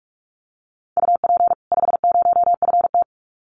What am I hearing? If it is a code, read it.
Morse code, heard as a keyed tone, about 44 wpm, 700 Hz. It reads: UP 50FT
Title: Common QSO Phrases 45wpm